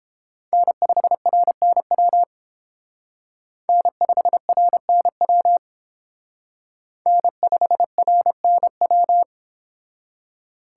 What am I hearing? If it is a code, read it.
Morse code, heard as a keyed tone, about 30 wpm, 700 Hz. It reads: N5RNW N5RNW N5RNW